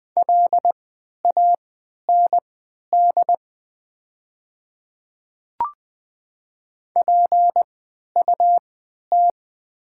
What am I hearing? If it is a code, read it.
Morse code, heard as a keyed tone, about 20 wpm, spaced as for 11 wpm, 700 Hz. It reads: LAND PUT
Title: 3x Character Spacing - Top 300 Words - 20wpm